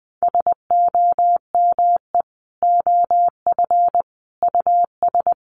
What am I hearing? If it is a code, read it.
Morse code, heard as a keyed tone, about 20 wpm, 700 Hz. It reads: SOME OF US
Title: Sentences from Top 1400 Words 20wpm